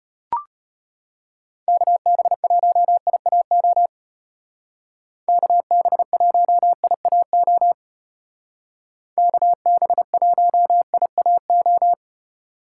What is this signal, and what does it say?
Morse code, heard as a keyed tone, about 31 wpm, 700 Hz. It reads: KB1IAO KB1IAO KB1IAO
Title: Speed Racing - USA 2x3 Call Signs 25wpm